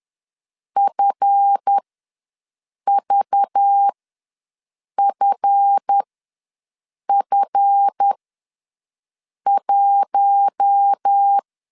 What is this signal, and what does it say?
Morse code, heard as a keyed tone, about 10 wpm, spaced as for 6 wpm, 800 Hz. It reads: FVFF1
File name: LEZIONI / 7 GF9Z / 7 GF ws 10wpm 3918